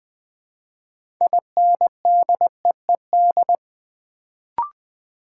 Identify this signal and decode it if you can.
Morse code, heard as a keyed tone, about 20 wpm, 690 Hz. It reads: INDEED
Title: Build A Word - 3 Letters At A Time - 6-Letter Words - 20wpm